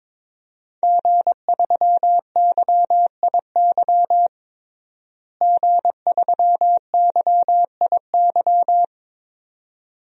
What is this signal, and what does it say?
Morse code, heard as a keyed tone, about 22 wpm, 700 Hz. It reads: G3YIY G3YIY
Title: DXCC-100 Call Signs - 5 Characters - Repeated 2x - 22wpm